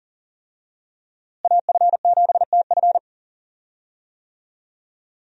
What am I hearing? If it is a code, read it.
Morse code, heard as a keyed tone, about 40 wpm, 700 Hz. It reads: AF7TF
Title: USA Call Signs - 5 Characters - Repeated 2x - 40wpm